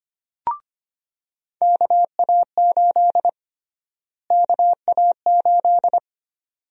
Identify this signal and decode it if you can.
Morse code, heard as a keyed tone, about 25 wpm, 700 Hz. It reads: KA8 KA8
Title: USA Call Sign Prefixes - Repeated 2x - 25wpm